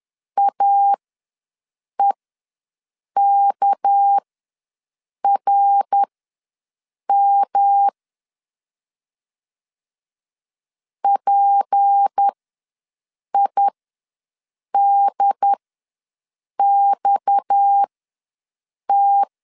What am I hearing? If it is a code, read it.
Morse code, heard as a keyed tone, about 10 wpm, spaced as for 6 wpm, 800 Hz. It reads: AEKRM PIDXT
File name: LEZIONI / 6 PKX0 / 6 PKX ws 10wpm 3706